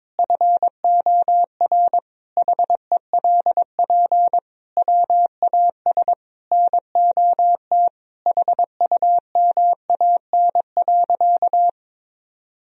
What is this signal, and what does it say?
Morse code, heard as a keyed tone, about 22 wpm, 700 Hz. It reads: FOR HELP WAS NOT HUMAN.
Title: Sentences from Top 1000 Words - Review Part 3 - 22wpm